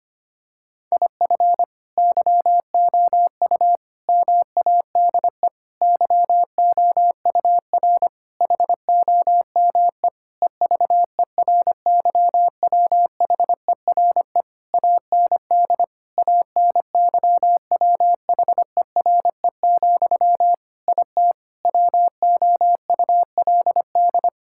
Morse code, heard as a keyed tone, about 25 wpm, 700 Hz. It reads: IF YOU MADE YOUR HOME EVERYWHERE AND ANYWHERE, IT WOULD
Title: Sentences from Top 1600 Words 25wpm